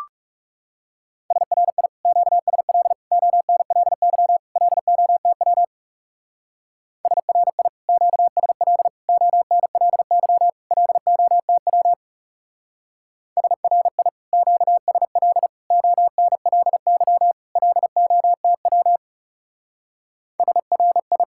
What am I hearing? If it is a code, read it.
Morse code, heard as a keyed tone, about 35 wpm, 700 Hz. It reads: SRI QSL ONLY LOTW SRI QSL ONLY LOTW SRI QSL ONLY LOTW SRI
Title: Speed-Racing Common QSO Phrases - 30wpm